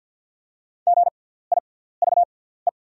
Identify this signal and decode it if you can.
Morse code, heard as a keyed tone, about 48 wpm, spaced as for 17 wpm, 690 Hz. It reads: GIVE
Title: 6x Character Spacing - Top 200 Words - 50wpm